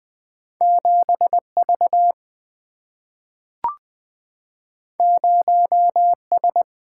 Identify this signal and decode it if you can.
Morse code, heard as a keyed tone, about 20 wpm, 700 Hz. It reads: 7V 0S
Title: Single-Letter Single-Number 20wpm